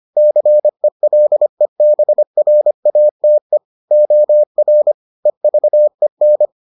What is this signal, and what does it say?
Morse code, heard as a keyed tone, about 25 wpm, 600 Hz. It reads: CELEBRATE OR EVEN